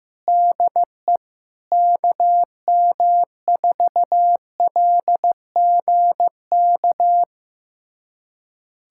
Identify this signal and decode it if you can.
Morse code, heard as a keyed tone, about 15 wpm, 700 Hz. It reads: DE KM4LGK
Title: USA Callsign DE Callsign - 6 Characters - 15wpm